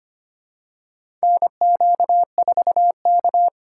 Morse code, heard as a keyed tone, about 25 wpm, 700 Hz. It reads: NQ4K
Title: USA Call Signs - 4 Characters - 6x Character Spacing - 25wpm